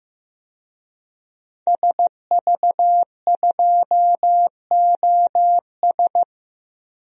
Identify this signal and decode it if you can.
Morse code, heard as a keyed tone, about 15 wpm, 700 Hz. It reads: SV2OS
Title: DXCC-100 Call Signs - 5 Characters - 15wpm